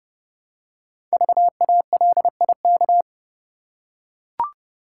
Morse code, heard as a keyed tone, about 29 wpm, 700 Hz. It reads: VALIK
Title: Code Groups - 5 Letters - 30wpm